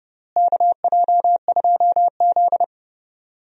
Morse code, heard as a keyed tone, about 30 wpm, 700 Hz. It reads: KJ2Z
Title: USA Call Signs - 4 Characters - Repeated 3x - 30wpm